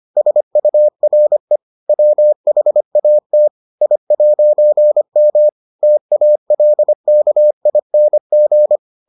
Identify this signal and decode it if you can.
Morse code, heard as a keyed tone, about 25 wpm, 600 Hz. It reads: SURE WHAT I'M TALKING